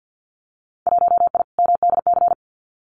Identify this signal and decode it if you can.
Morse code, heard as a keyed tone, about 50 wpm, 700 Hz. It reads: 1I MDC
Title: Winter Field Day Exchanges 50wpm